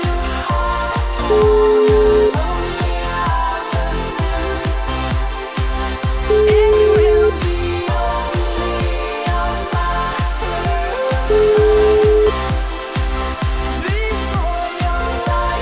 Ringback tone (Europe, 425 Hz), with noise.